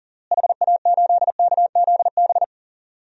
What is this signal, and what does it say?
Morse code, heard as a keyed tone, about 40 wpm, 700 Hz. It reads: HA8KZB
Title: DXCC-100 Call Signs - 6 Characters - Repeated 3x - 40wpm